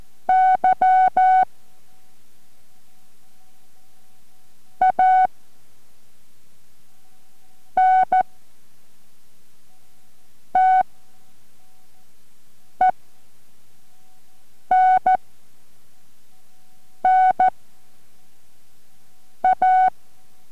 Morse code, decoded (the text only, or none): Y ANTENNA